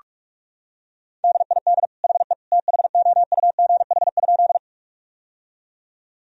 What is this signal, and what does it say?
Morse code, heard as a keyed tone, about 46 wpm, 700 Hz. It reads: DID HE THOUGH?
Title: Sentences from Top 1000 Words - Review Part 2 - 45wpm